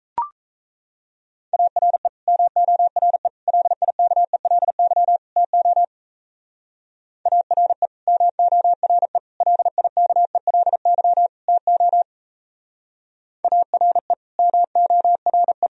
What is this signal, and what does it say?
Morse code, heard as a keyed tone, about 35 wpm, 700 Hz. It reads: ARE MORE LIKELY TO ARE MORE LIKELY TO ARE MORE
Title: Speed-Racing - Sets of 4 Words 28wpm